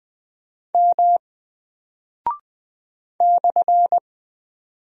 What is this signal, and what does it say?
Morse code, heard as a keyed tone, about 20 wpm, 700 Hz. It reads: M /
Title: Single Character (Letter:Number:Punctuation:Prosign) Alt 20wpm